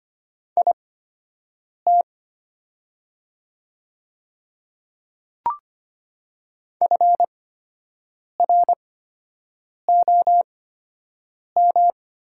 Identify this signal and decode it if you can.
Morse code, heard as a keyed tone, about 25 wpm, spaced as for 7 wpm, 700 Hz. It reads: IT FROM
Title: 8x Character Spacing - Top 200 Words - 25wpm